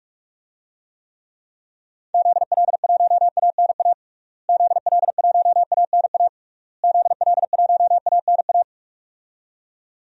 Morse code, heard as a keyed tone, about 46 wpm, 700 Hz. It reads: ZL1ANA ZL1ANA ZL1ANA
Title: DXCC-100 Call Signs - 6 Characters - Repeated 3x - 45wpm